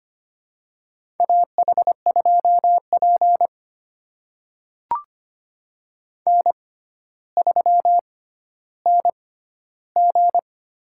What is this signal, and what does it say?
Morse code, heard as a keyed tone, about 25 wpm, spaced as for 13 wpm, 700 Hz. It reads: AH2P N3NG
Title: USA Call Signs - 4 Characters - 6x Character Spacing - 25wpm